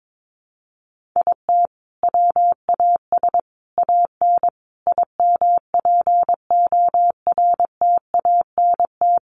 Morse code, heard as a keyed tone, about 22 wpm, 700 Hz. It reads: IT WAS AN IMPORTANT
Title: Sentences from Top 1400 Words 22wpm